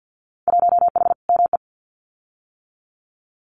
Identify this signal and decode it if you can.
Morse code, heard as a keyed tone, about 50 wpm, 700 Hz. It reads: 1H ME
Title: Winter Field Day Exchanges 50wpm